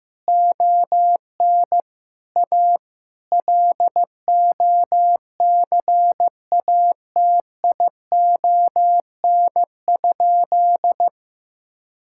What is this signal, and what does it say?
Morse code, heard as a keyed tone, about 16 wpm, 700 Hz. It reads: ON A LOCATION?
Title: Sentences from Top 1000 Words - Review Part 6 - 15wpm